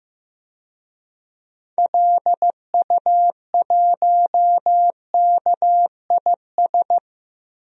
Morse code, heard as a keyed tone, about 15 wpm, 700 Hz. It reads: LU1KIS
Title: DXCC-100 Call Signs - 6 Characters - Repeated 2x - 15wpm